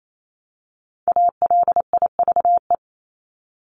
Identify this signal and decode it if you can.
Morse code, heard as a keyed tone, about 28 wpm, 700 Hz. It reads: ALIVE
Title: Top 1751-2000 Words 28wpm